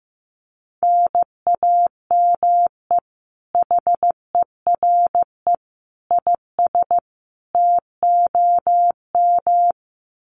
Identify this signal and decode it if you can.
Morse code, heard as a keyed tone, about 15 wpm, 700 Hz. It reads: NAME HERE IS TOM
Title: Common QSO Phrases 15wpm